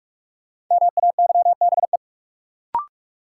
Morse code, heard as a keyed tone, about 46 wpm, 700 Hz. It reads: MAYBE
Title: Sentences from Top 1000 Words - New Word Drill - 45wpm